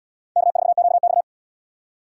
Morse code, heard as a keyed tone, about 76 wpm, 700 Hz. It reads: K4QX